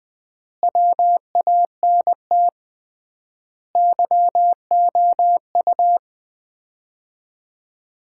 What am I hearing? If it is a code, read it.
Morse code, heard as a keyed tone, about 20 wpm, 700 Hz. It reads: WANT YOU
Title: 3x Word-Spacing - Sets of 5 Words - 20wpm